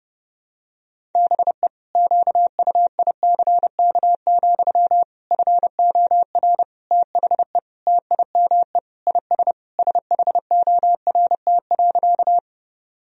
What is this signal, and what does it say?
Morse code, heard as a keyed tone, about 30 wpm, 700 Hz. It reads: BE QUICK, FOR THE TIME IS SHORT.
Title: Sentences from Top 1000 Words - Review Part 2 - 30wpm